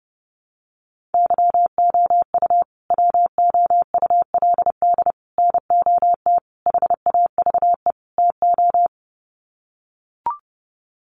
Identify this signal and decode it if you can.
Morse code, heard as a keyed tone, about 29 wpm, 700 Hz. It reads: YOU WOULD NOT HAVE TO
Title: Sets of 5 Words Encore 30wpm